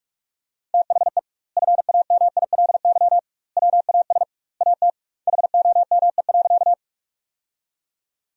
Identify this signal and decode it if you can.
Morse code, heard as a keyed tone, about 48 wpm, 700 Hz. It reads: THE FAMILY WAS AT HOME.
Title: 2x Word-Spacing - Sentences from Top 400 Words - 45wpm